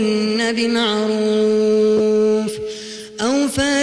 Congestion tone (425 Hz), with noise.